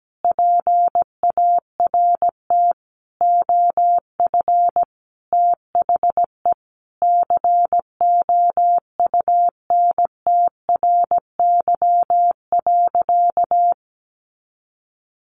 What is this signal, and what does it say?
Morse code, heard as a keyed tone, about 17 wpm, 700 Hz. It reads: PART OF THE COUNTRY.
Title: Sentences from Top 300 Words 17wpm